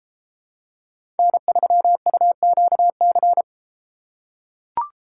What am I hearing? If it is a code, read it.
Morse code, heard as a keyed tone, about 32 wpm, 700 Hz. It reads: N3UQC